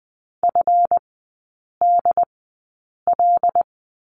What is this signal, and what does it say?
Morse code, heard as a keyed tone, about 20 wpm, spaced as for 8 wpm, 700 Hz. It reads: FDL
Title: 145 Lesson 24e - Letter y - Three Characters 20wpm